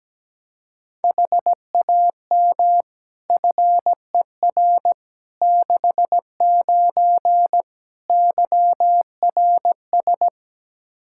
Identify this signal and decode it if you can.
Morse code, heard as a keyed tone, about 17 wpm, 700 Hz. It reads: HAM FER 69 YRS
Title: K1USN SST QSO Practice 17wpm